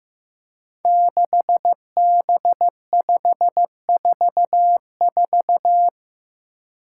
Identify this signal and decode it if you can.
Morse code, heard as a keyed tone, about 15 wpm, 700 Hz. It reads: 6B544
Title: Five Challenging Letter-Number Combinations 15wpm